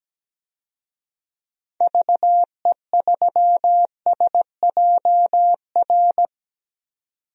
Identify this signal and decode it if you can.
Morse code, heard as a keyed tone, about 17 wpm, 700 Hz. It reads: VE3SJR
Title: DXCC-100 Call Signs - 6 Characters - 17wpm